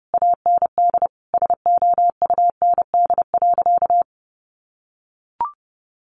Morse code, heard as a keyed tone, about 30 wpm, 700 Hz. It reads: AND SOUND.
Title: Sentences from Top 900 Words 30wpm